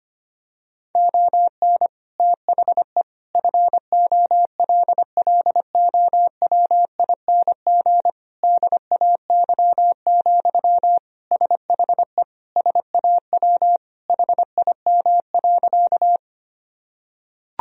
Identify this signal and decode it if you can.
Morse code, heard as a keyed tone, about 25 wpm, 700 Hz. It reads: ON THE FOLLOWING DAY, SHE SAW HIM.
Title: Sentences from Top 1000 Words - Review Part 4 - 25wpm